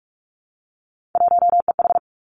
Morse code, heard as a keyed tone, about 46 wpm, 700 Hz. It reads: 1EH
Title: Three Challenging Letter-Number Combinations 45wpm